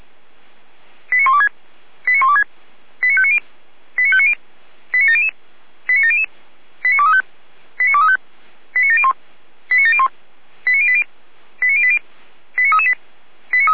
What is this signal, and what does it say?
Five-tone selective call (ZVEI), heard as quick runs of five tones, tones of about 70 ms. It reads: 89126 89126 89590 89590 89700 89700 89235 89235 89701 89701 89080 89080 89338